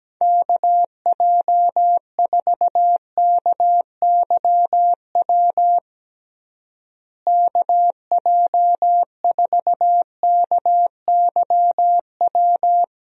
Morse code, heard as a keyed tone, about 17 wpm, 700 Hz. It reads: KJ4KYW KJ4KYW
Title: USA Call Signs - 6 Characters - Repeated 3x - 17wpm